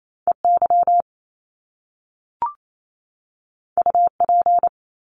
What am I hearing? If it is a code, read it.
Morse code, heard as a keyed tone, about 28 wpm, 700 Hz. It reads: EY UP
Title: Top Two Letter Combinations in Words 28wpm